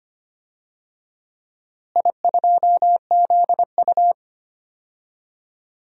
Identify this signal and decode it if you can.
Morse code, heard as a keyed tone, about 25 wpm, 700 Hz. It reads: I2ZU
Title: DXCC-100 Call Signs - 4 Characters - Repeated 3x - 25wpm